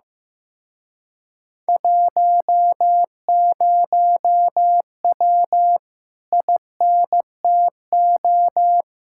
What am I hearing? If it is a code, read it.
Morse code, heard as a keyed tone, about 15 wpm, 700 Hz. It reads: 10W INTO
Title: Speed-Racing Common QSO Phrases Encore - 15wpm